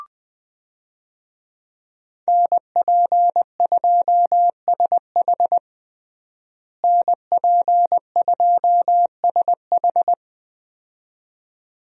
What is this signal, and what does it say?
Morse code, heard as a keyed tone, about 20 wpm, 700 Hz. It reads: NP2SH NP2SH
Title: USA Call Signs - 5 Characters - Repeated 2x - 20wpm